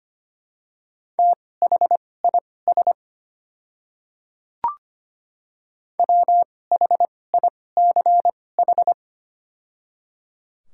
Morse code, heard as a keyed tone, about 25 wpm, spaced as for 18 wpm, 700 Hz. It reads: THIS WHICH
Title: 2x Character Spacing - Top 300 Words - 25wpm